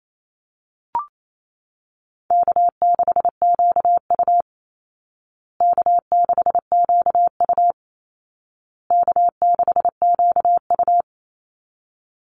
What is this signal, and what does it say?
Morse code, heard as a keyed tone, about 28 wpm, 700 Hz. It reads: K6QU K6QU K6QU
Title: Call Signs Repeated 2x - 28wpm